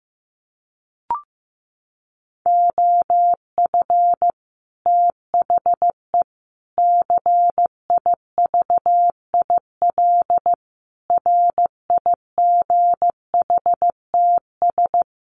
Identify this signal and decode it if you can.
Morse code, heard as a keyed tone, about 15 wpm, 700 Hz. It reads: OF THE CIVIL RIGHTS
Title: Sets of 5 Words Encore2 15wpm